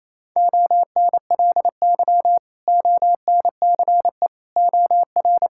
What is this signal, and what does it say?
Morse code, heard as a keyed tone, about 27 wpm, 700 Hz. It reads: ONLY ONCE OR
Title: Sentences from Top 1600 Words 28wpm